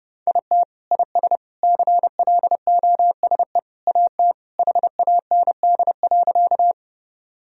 Morse code, heard as a keyed tone, about 30 wpm, 700 Hz. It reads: IT IS CLOSE AT HAND.
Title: Sentences from Top 1000 Words - Review Part 1 - 30wpm